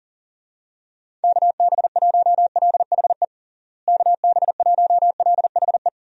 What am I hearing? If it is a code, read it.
Morse code, heard as a keyed tone, about 40 wpm, 700 Hz. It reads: KB1LHE KB1LHE
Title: USA Call Signs - 6 Characters - Repeated 2x - 40wpm